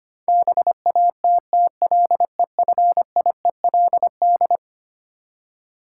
Morse code, heard as a keyed tone, about 25 wpm, 700 Hz. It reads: BATTLEFIELD
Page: morse-code-ninja-practice-25wpm